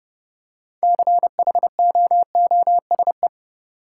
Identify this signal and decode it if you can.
Morse code, heard as a keyed tone, about 30 wpm, 700 Hz. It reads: CHOOSE
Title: Six Letter Words - 30wpm